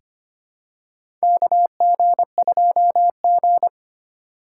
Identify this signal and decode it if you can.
Morse code, heard as a keyed tone, about 25 wpm, 700 Hz. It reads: KG2G
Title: USA Call Signs - 4 Characters - 4x Character Spacing - 25wpm